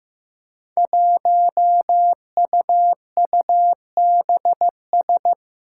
Morse code, heard as a keyed tone, about 15 wpm, 700 Hz. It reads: 1UUBS
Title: Five Challenging Letter-Number Combinations 15wpm